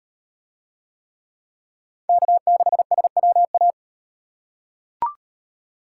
Morse code, heard as a keyed tone, about 35 wpm, 700 Hz. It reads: K6SWA